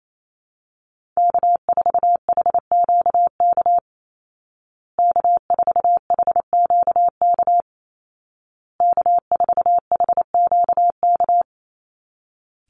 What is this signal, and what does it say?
Morse code, heard as a keyed tone, about 28 wpm, 700 Hz. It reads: K4HQK K4HQK K4HQK
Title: Call Signs Repeated 2x - 28wpm